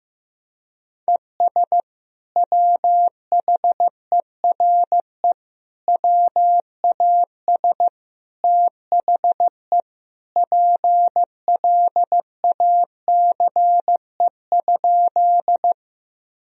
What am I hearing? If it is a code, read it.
Morse code, heard as a keyed tone, about 15 wpm, 700 Hz. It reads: ES WHERE WAS THE PLACE?
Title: Sentences from Top 200 Words and Abbreviations 15wpm